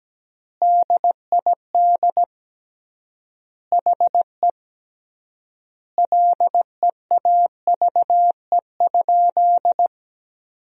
Morse code, heard as a keyed tone, about 17 wpm, 700 Hz. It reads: DID HE LEAVE?
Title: 3x Word-Spacing - Sentences from Top 400 Words - 17wpm